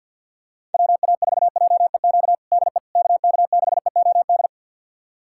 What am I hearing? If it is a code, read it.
Morse code, heard as a keyed tone, about 51 wpm, 700 Hz. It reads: WA4JEQ DE KK6EOD
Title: USA Callsign DE Callsign 50wpm